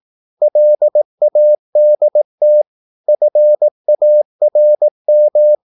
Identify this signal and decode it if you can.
Morse code, heard as a keyed tone, about 18 wpm, 600 Hz. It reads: LADT FARM
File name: Korte ord DK 18wpm